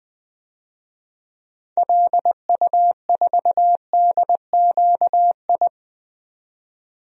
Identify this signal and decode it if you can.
Morse code, heard as a keyed tone, about 20 wpm, 700 Hz. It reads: LU4DQI
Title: DXCC-100 Call Signs - 6 Characters - 20wpm